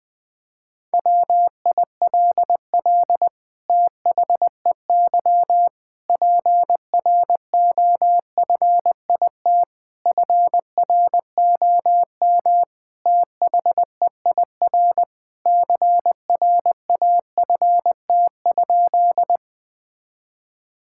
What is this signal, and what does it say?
Morse code, heard as a keyed tone, about 20 wpm, 700 Hz. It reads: WILL THEY PROFIT FROM THEIR CRAFT?
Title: Sentences from Top 1000 Words - Review Part 6 - 20wpm